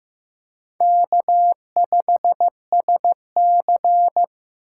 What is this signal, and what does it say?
Morse code, heard as a keyed tone, about 15 wpm, 700 Hz. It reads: K5SC
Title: USA Call Signs - 3x Character Spacing - 15wpm